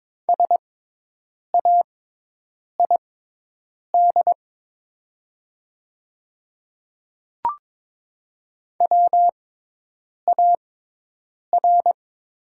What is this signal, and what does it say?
Morse code, heard as a keyed tone, about 22 wpm, spaced as for 8 wpm, 690 Hz. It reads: SAID WAR
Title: 6x Character Spacing - Top 500 Words - 22wpm